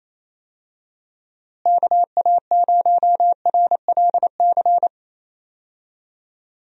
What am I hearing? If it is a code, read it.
Morse code, heard as a keyed tone, about 28 wpm, 700 Hz. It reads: KA0RLC
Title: USA Call Signs - 6 Characters - Repeated 3x - 28wpm